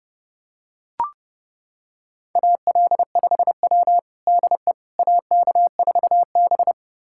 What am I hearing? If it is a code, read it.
Morse code, heard as a keyed tone, about 30 wpm, 700 Hz. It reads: AL5W DE AK4B
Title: USA Callsign DE Callsign - 4 Characters - 30wpm